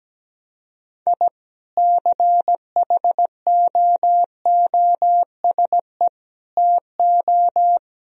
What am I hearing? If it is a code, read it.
Morse code, heard as a keyed tone, about 17 wpm, 700 Hz. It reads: I CHOOSE TO
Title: Sentences from Top 1000 Words - Review Part 4 - 17wpm